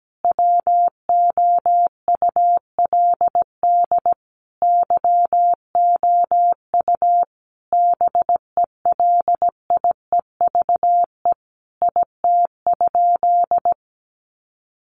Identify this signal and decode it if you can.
Morse code, heard as a keyed tone, about 17 wpm, 700 Hz. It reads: WOULD YOU BELIEVE IT?
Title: Sentences from Top 1000 Words 17wpm